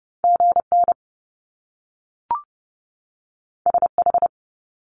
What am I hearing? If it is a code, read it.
Morse code, heard as a keyed tone, about 30 wpm, 700 Hz. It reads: GN SH
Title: Top Two Letter Combinations in Words 30wpm